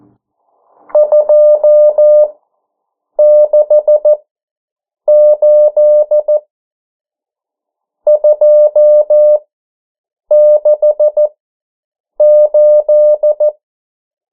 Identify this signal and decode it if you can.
Morse code, heard as a keyed tone, about 14 wpm, spaced as for 7 wpm, 600 Hz. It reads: 268 268
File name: morse mp3 / 09 urok 8